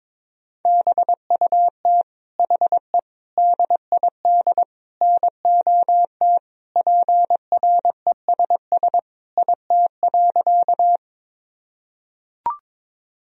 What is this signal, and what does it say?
Morse code, heard as a keyed tone, about 22 wpm, 700 Hz. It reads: BUT HE DID NOT PRESS IT.
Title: Sentences from Top 1000 Words - Review Part 1 - 22wpm